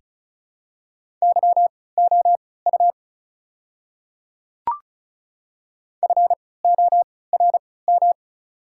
Morse code, heard as a keyed tone, about 36 wpm, spaced as for 20 wpm, 690 Hz. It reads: YOU FORM
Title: 3x Character Spacing - Top 200 Words - 35wpm